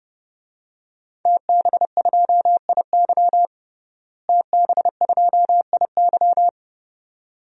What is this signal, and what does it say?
Morse code, heard as a keyed tone, about 30 wpm, 700 Hz. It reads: TB2IY TB2IY
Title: DXCC-100 Call Signs - 5 Characters - Repeated 2x - 30wpm